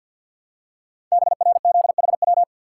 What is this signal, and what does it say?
Morse code, heard as a keyed tone, about 51 wpm, 700 Hz. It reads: BRZHW
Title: Code Groups - 5 Letters - 50wpm